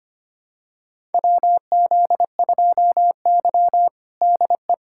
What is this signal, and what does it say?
Morse code, heard as a keyed tone, about 25 wpm, 700 Hz. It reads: WZ2Y DE
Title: USA Callsign DE Callsign - 4 Characters - 25wpm